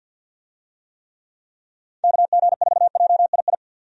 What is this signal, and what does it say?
Morse code, heard as a keyed tone, about 49 wpm, 700 Hz. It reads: KG4JII
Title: USA Call Signs - 6 Characters - 50wpm